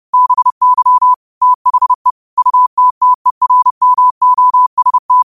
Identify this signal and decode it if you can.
Morse code, heard as a keyed tone, about 30 wpm, 1000 Hz. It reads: BY THE UTTERMOST
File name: qotdfiles / 30wpm